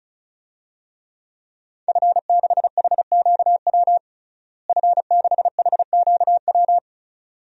Morse code, heard as a keyed tone, about 34 wpm, 700 Hz. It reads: F6HQW F6HQW
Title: DXCC-100 Call Signs - 5 Characters - Repeated 2x - 35wpm